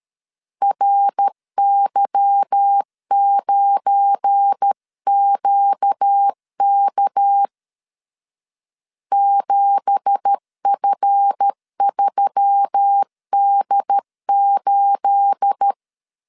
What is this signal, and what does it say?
Morse code, heard as a keyed tone, about 13 wpm, 800 Hz. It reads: RY9QK 7F3D8
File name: CW / DET / 1 9 1500C S C 12wpm 3811